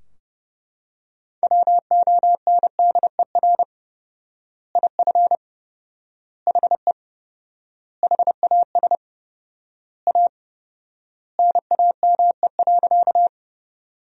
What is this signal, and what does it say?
Morse code, heard as a keyed tone, about 31 wpm, 700 Hz. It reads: WONDER IF HE HAS A NAME.
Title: 4x Word-Spacing - Sentences from Top 500 Words - 30wpm